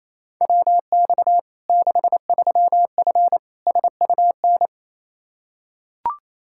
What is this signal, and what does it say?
Morse code, heard as a keyed tone, about 28 wpm, 700 Hz. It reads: WX 63F SUN
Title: Practice QSO - Exchange 2 - Info - 28wpm